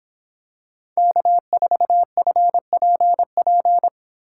Morse code, heard as a keyed tone, about 26 wpm, 700 Hz. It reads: K4FPP